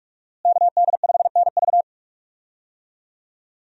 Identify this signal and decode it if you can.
Morse code, heard as a keyed tone, about 43 wpm, 700 Hz. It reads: KD5NV
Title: USA Call Signs - 5 Characters - Repeated 3x - 45wpm